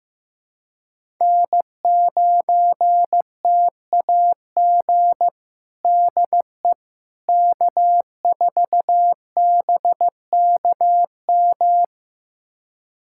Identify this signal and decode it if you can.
Morse code, heard as a keyed tone, about 15 wpm, 700 Hz. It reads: N9TAG DE K4BKM
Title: USA Callsign DE Callsign - 5 Characters - 15wpm